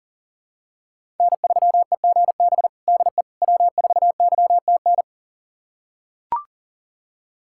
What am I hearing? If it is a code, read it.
Morse code, heard as a keyed tone, about 40 wpm, 700 Hz. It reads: N3EGB DE W4YTN
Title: USA Callsign DE Callsign - 5 Characters - 40wpm